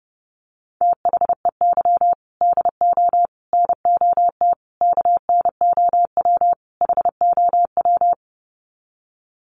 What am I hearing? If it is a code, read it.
Morse code, heard as a keyed tone, about 30 wpm, 700 Hz. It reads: THEY DO NOT KNOW HOW
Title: Sets of 5 Words Encore2 30wpm